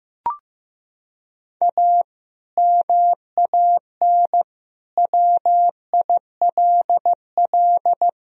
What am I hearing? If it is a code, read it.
Morse code, heard as a keyed tone, about 15 wpm, 700 Hz. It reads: A MAN WILL
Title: Sentences from Top 1000 Words - Review Part 3 - 15wpm